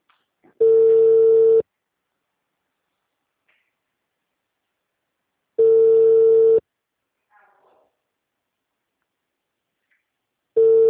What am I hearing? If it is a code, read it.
Ringback tone (China, 450 Hz).